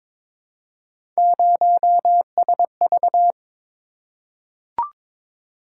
Morse code, heard as a keyed tone, about 22 wpm, 700 Hz. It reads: 0SV
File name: practice / 22wpm / Three Letter-Number Combinations - 22wpm